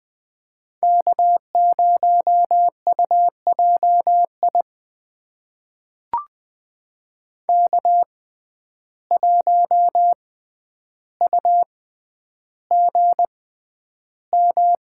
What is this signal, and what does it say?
Morse code, heard as a keyed tone, about 20 wpm, spaced as for 10 wpm, 700 Hz. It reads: K0UJI K1UGM
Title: USA Call Signs - 5 Characters - 6x Character Spacing - 20wpm